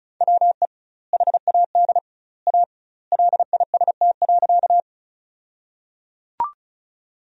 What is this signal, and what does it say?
Morse code, heard as a keyed tone, about 34 wpm, 700 Hz. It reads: WE HAD A LIST.
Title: 2x Word-Spacing - Sentences from Top 400 Words - 35wpm